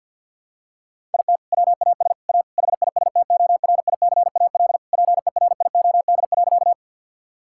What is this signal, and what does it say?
Morse code, heard as a keyed tone, about 50 wpm, 700 Hz. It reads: IT WAS A HISTORICAL PERIOD.
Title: Sentences from Top 1000 Words - Review Part 5 - 50wpm